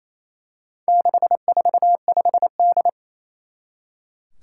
Morse code, heard as a keyed tone, about 27 wpm, 700 Hz. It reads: 645D
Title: Four Challenging Letter-Number Combinations 28wpm